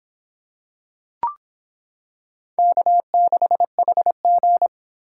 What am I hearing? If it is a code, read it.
Morse code, heard as a keyed tone, about 26 wpm, 700 Hz. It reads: K6HG